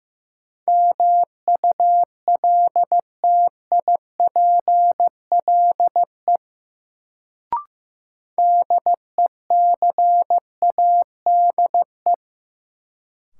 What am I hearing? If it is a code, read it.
Morse code, heard as a keyed tone, about 15 wpm, 700 Hz. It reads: MULTIPLE DECADE
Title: Sets of 5 Words - New Word Drill - 15wpm